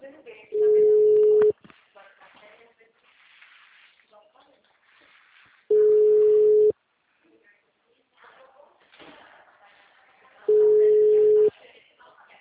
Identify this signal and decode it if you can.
Ringback tone (Europe, 425 Hz), with speech.
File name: out- -2011- - - .13060